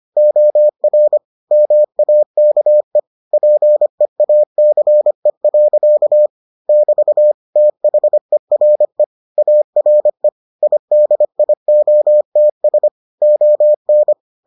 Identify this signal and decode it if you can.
Morse code, heard as a keyed tone, about 25 wpm, 600 Hz. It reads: OR MAKE PEACE. <BT> THERE ARE IDIOTS ON